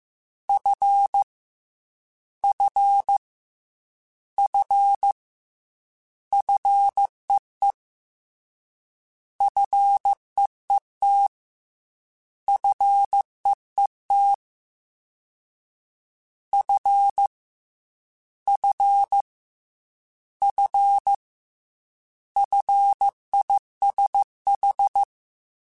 Morse code, heard as a keyed tone, about 15 wpm, 800 Hz. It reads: F F F FEE FEET FEET F F F FISH
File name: F and L / F and L mp3 / F and L Prefixes 3x2 Twice @13